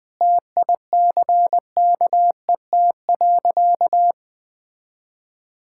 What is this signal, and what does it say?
Morse code, heard as a keyed tone, about 20 wpm, 700 Hz. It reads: TICKET.
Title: Sentences with 6-Word Phrases - Spoken Beginning - 20wpm